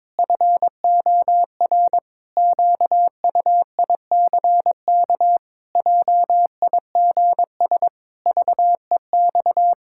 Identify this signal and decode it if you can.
Morse code, heard as a keyed tone, about 22 wpm, 700 Hz. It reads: FOR QUICK JIGS VEX
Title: Pangram Challenge v2 - 22wpm